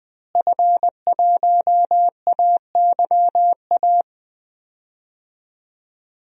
Morse code, heard as a keyed tone, about 20 wpm, 700 Hz. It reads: F1AYA
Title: DXCC-100 Call Signs - 5 Characters - Repeated 2x - 20wpm